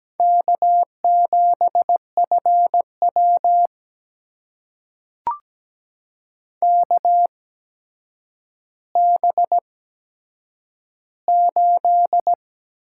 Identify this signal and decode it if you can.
Morse code, heard as a keyed tone, about 17 wpm, spaced as for 8 wpm, 700 Hz. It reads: K7FW KB8
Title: USA Call Signs - 4 Characters - 8x Character Spacing - 17wpm